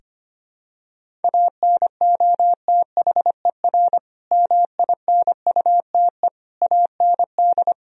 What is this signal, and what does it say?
Morse code, heard as a keyed tone, about 25 wpm, 700 Hz. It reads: ANOTHER MINUTE AND
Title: Sentences from Top 1000 Words - Review Part 3 - 25wpm